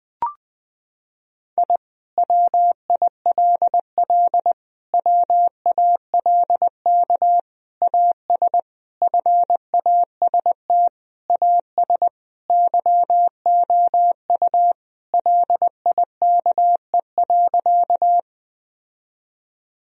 Morse code, heard as a keyed tone, about 20 wpm, 700 Hz. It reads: I WILL WALK AS FAST AS YOU LIKE.
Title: Sentences from Top 1000 Words - Review Part 2 - 20wpm